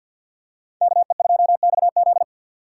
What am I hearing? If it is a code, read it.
Morse code, heard as a keyed tone, about 46 wpm, 700 Hz. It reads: KE2XZ
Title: USA Call Signs - 3x Character Spacing - 50wpm